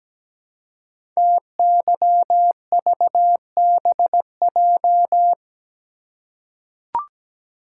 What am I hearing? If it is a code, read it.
Morse code, heard as a keyed tone, about 17 wpm, 700 Hz. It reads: TYVBJ
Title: Code Groups - 5 Letters - 17wpm